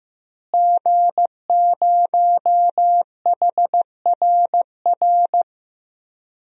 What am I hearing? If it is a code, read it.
Morse code, heard as a keyed tone, about 15 wpm, 700 Hz. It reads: G0HRR
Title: DXCC-100 Call Signs - 5 Characters - Repeated 3x - 15wpm